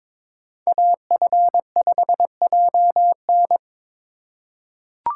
Morse code, heard as a keyed tone, about 23 wpm, 700 Hz. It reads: AF5JN
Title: USA Call Signs - 5 Characters - 3x Character Spacing - 22wpm